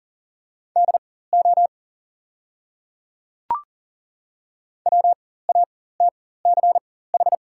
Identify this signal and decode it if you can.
Morse code, heard as a keyed tone, about 41 wpm, spaced as for 19 wpm, 700 Hz. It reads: DO WATCH
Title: 4x Character Spacing - Top 300 Words - 40wpm